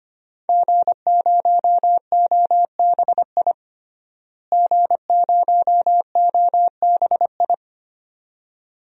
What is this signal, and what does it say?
Morse code, heard as a keyed tone, about 25 wpm, 700 Hz. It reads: G0OBI G0OBI
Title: DXCC-100 Call Signs - 5 Characters - Repeated 2x - 25wpm